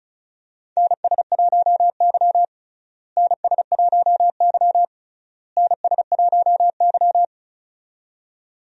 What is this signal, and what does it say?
Morse code, heard as a keyed tone, about 36 wpm, 700 Hz. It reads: NS1Y NS1Y NS1Y
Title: USA Call Signs - 4 Characters - Repeated 3x - 35wpm